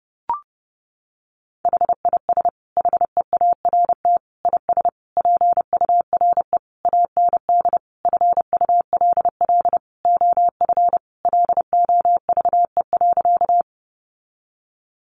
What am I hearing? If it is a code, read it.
Morse code, heard as a keyed tone, about 30 wpm, 700 Hz. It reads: HIS HEART IS PURE AND FULL OF LOVE.
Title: Sentences from Top 1500 Words 30wpm